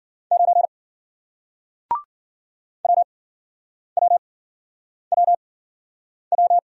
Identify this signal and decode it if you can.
Morse code, heard as a keyed tone, about 55 wpm, spaced as for 9 wpm, 700 Hz. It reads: 9 WWWW